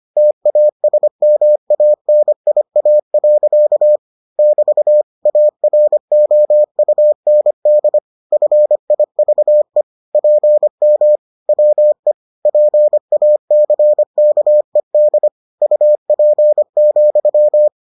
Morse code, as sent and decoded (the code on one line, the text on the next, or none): - .- ... -- .- -. .. .- .-.-.- / -...- / .- .-. --- ..- -. -.. / ..-. .. ...- . / .--. -- / .-- . / .--. .- -.-. -.- . -.. / ..- .--. --..--
TASMANIA. <BT> AROUND FIVE PM WE PACKED UP,